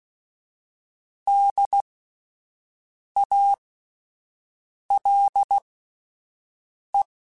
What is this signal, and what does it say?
Morse code, heard as a keyed tone, about 16 wpm, spaced as for 5 wpm, 800 Hz. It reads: DALE